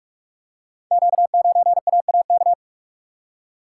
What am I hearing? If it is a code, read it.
Morse code, heard as a keyed tone, about 44 wpm, 690 Hz. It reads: Q9AAK
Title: Code Groups - 5 Letter:Numbers - 45wpm